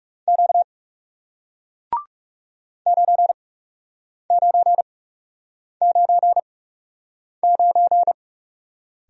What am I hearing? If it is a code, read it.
Morse code, heard as a keyed tone, about 40 wpm, spaced as for 8 wpm, 700 Hz. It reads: Q 9999